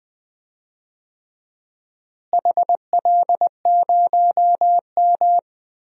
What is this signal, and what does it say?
Morse code, heard as a keyed tone, about 20 wpm, 700 Hz. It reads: HL0M
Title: DXCC-100 Call Signs - 4 Characters - Repeated 3x - 20wpm